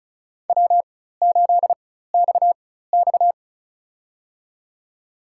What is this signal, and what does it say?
Morse code, heard as a keyed tone, about 36 wpm, spaced as for 16 wpm, 700 Hz. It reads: W8XX
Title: USA Call Signs - 4 Characters - 4x Character Spacing - 35wpm